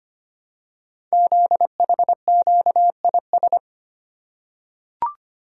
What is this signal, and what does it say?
Morse code, heard as a keyed tone, about 25 wpm, 700 Hz. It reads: ZHQIS
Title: Code Groups - 5 Letters - 25wpm